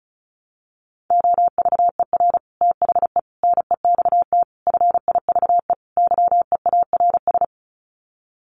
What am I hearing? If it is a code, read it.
Morse code, heard as a keyed tone, about 34 wpm, 700 Hz. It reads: OVER THE NEXT FIVE YEARS
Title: Sets of 5 Words 35wpm